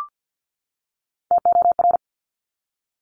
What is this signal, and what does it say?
Morse code, heard as a keyed tone, about 50 wpm, 700 Hz. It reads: TOR
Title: Top Three Letter Combinations in Words 50wpm